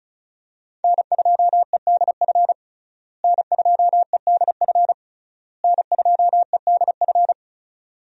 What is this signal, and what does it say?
Morse code, heard as a keyed tone, about 34 wpm, 700 Hz. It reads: N2EDF N2EDF N2EDF
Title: USA Call Signs - 5 Characters - Repeated 3x - 35wpm